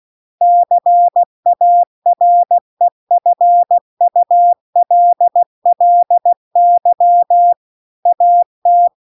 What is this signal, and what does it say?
Morse code, heard as a keyed tone, about 16 wpm, 700 Hz. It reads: CAREFULLY AT